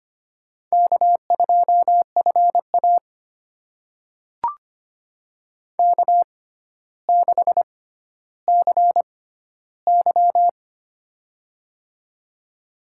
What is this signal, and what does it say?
Morse code, heard as a keyed tone, about 25 wpm, spaced as for 13 wpm, 700 Hz. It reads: K2FA K6CY
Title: USA Call Signs - 4 Characters - 6x Character Spacing - 25wpm